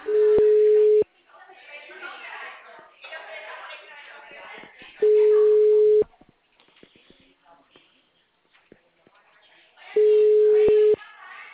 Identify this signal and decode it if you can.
Ringback tone (Europe, 425 Hz), with speech.